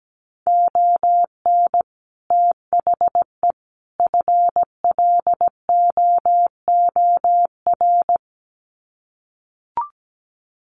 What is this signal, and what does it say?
Morse code, heard as a keyed tone, about 17 wpm, 700 Hz. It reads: ON THE FLOOR
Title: Sets of 3 Words 17wpm